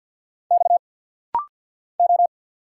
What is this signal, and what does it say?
Morse code, heard as a keyed tone, about 50 wpm, 700 Hz. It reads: X O
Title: Single Letters - Rapid-Fire 50wpm